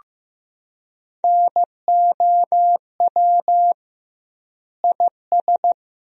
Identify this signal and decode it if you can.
Morse code, heard as a keyed tone, about 16 wpm, 700 Hz. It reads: NOW IS
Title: 2x Word-Spacing - Sentences from Top 100 Words - 15wpm